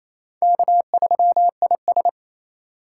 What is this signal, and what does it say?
Morse code, heard as a keyed tone, about 34 wpm, 700 Hz. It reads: K3IS
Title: Speed Racing - USA 1x2 Call Signs 28wpm